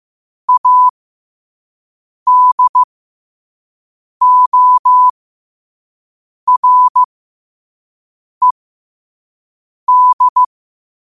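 Morse code, heard as a keyed tone, about 15 wpm, spaced as for 5 wpm, 1000 Hz. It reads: ADORED
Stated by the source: Morse code quote of the day at 5 WPM.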